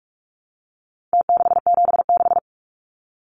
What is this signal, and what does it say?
Morse code, heard as a keyed tone, about 44 wpm, 700 Hz. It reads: T676
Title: Numbers - Contest Style - 4-Digit 45wpm